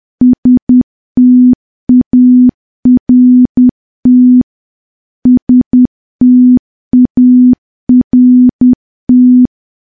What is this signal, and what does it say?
Morse code, heard as a keyed tone, about 10 wpm, 250 Hz. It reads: START START
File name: morse / morse-training-10WPM